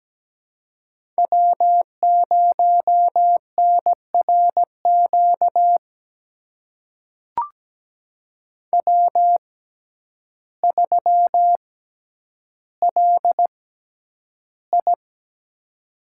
Morse code, heard as a keyed tone, about 17 wpm, spaced as for 9 wpm, 700 Hz. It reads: W0NRQ W3LI
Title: USA Call Signs - 6x Character Spacing - 17wpm